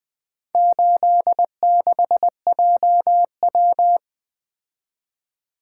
Morse code, heard as a keyed tone, about 20 wpm, 700 Hz. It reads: 86JW
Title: Four Letter-Number Combinations - 20wpm